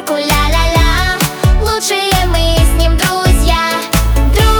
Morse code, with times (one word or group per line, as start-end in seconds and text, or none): none